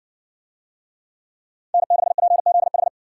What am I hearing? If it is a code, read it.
Morse code, heard as a keyed tone, about 60 wpm, 700 Hz. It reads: N6PZH